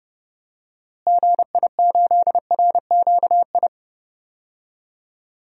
Morse code, heard as a keyed tone, about 30 wpm, 700 Hz. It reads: GI8RQI
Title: DXCC-100 Call Signs - 6 Characters - Repeated 2x - 30wpm